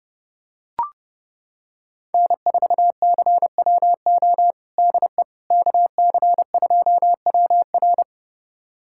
Morse code, heard as a keyed tone, about 30 wpm, 700 Hz. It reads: N4CWO DE KC2WR
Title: USA Callsign DE Callsign - 5 Characters - 30wpm